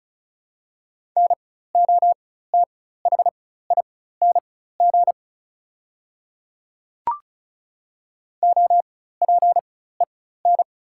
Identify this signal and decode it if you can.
Morse code, heard as a keyed tone, about 34 wpm, spaced as for 16 wpm, 690 Hz. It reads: NOTHING OPEN
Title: 4x Character Spacing - Top 500 Words - 35wpm